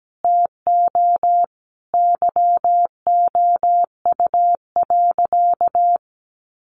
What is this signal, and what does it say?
Morse code, heard as a keyed tone, about 17 wpm, 700 Hz. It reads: TO YOU.
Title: Sentences from Top 1200 Words 17wpm